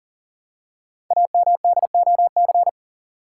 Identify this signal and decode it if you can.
Morse code, heard as a keyed tone, about 40 wpm, 700 Hz. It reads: AMDOC
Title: Code Groups - 5 Letters - 40wpm